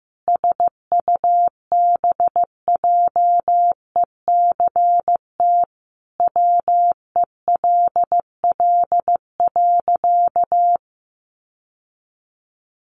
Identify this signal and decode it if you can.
Morse code, heard as a keyed tone, about 15 wpm, 700 Hz. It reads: SUBJECT WELL.
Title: Sentences from Top 800 Words 15wpm